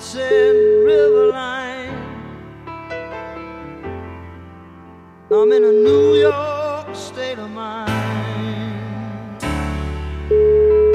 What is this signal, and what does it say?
Ringback tone (Europe, 425 Hz).